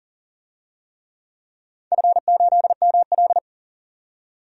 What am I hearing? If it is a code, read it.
Morse code, heard as a keyed tone, about 40 wpm, 700 Hz. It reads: F8ML
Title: DXCC-100 Call Signs - 4 Characters - 40wpm